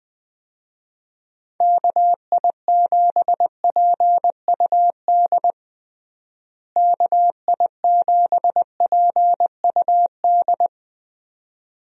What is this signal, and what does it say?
Morse code, heard as a keyed tone, about 21 wpm, 700 Hz. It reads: KI7PUD KI7PUD
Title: USA Call Signs - 6 Characters - Repeated 2x - 20wpm